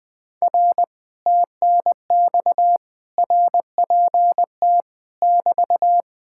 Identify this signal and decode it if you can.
Morse code, heard as a keyed tone, about 20 wpm, 700 Hz. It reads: R TNX RPT <BT>
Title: Speed-Racing Common QSO Phrases Encore - 17wpm